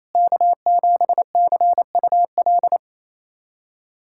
Morse code, heard as a keyed tone, about 28 wpm, 700 Hz. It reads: K7CUL
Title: USA Callsign DE Callsign 28wpm